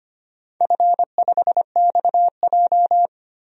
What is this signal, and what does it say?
Morse code, heard as a keyed tone, about 25 wpm, 690 Hz. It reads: F5XJ
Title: DXCC-100 Call Signs - 4 Characters - 25wpm